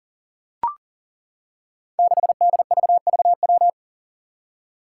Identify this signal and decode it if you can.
Morse code, heard as a keyed tone, about 40 wpm, 700 Hz. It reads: 6DVVW
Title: Five Challenging Letter-Number Combinations 40wpm